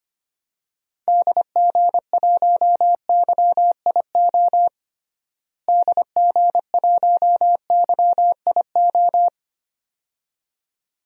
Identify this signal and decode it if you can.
Morse code, heard as a keyed tone, about 25 wpm, 700 Hz. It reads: DG1YIO DG1YIO
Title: DXCC-100 Call Signs - 6 Characters - Repeated 2x - 25wpm